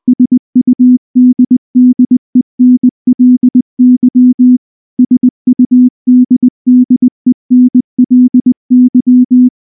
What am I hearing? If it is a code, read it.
Morse code, heard as a keyed tone, about 19 wpm, 250 Hz. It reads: SUDDENLY SUDDENLY